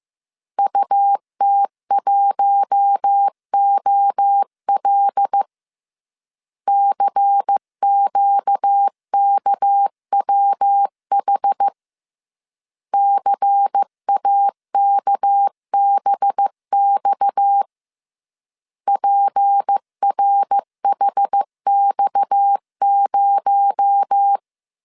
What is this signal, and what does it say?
Morse code, heard as a keyed tone, about 15 wpm, 800 Hz. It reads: UT1OL CQKWH CAKBX PRHX0
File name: CW / DET / 1 6 1500C S C 14wpm 2951